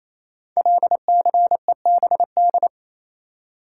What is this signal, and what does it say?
Morse code, heard as a keyed tone, about 28 wpm, 700 Hz. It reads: LCEBD
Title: Code Groups - 5 Letters - 28wpm